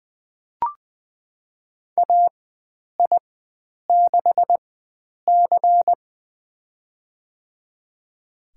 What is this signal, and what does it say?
Morse code, heard as a keyed tone, about 20 wpm, spaced as for 9 wpm, 700 Hz. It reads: AI6C
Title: USA Call Signs - 4x Character Spacing - 20wpm